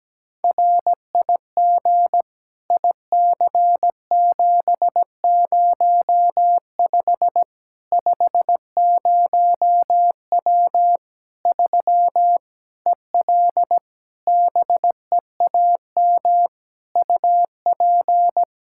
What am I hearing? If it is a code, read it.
Morse code, heard as a keyed tone, about 17 wpm, 700 Hz. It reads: RIG IC705 50W 3 EL BEAM UP
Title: Practice QSO - Exchange 2 - Info - 17wpm